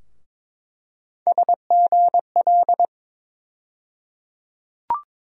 Morse code, heard as a keyed tone, about 22 wpm, 700 Hz. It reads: SGL
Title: USA Call Sign Postfixes - 3 Letters 22wpm